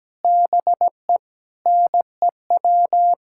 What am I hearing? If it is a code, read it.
Morse code, heard as a keyed tone, about 17 wpm, 700 Hz. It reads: BE NEW
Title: Sentences from Top 1600 Words 17wpm